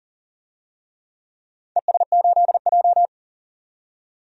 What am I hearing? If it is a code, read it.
Morse code, heard as a keyed tone, about 40 wpm, 700 Hz. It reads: ES8J
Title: DXCC-100 Call Signs - 4 Characters - 40wpm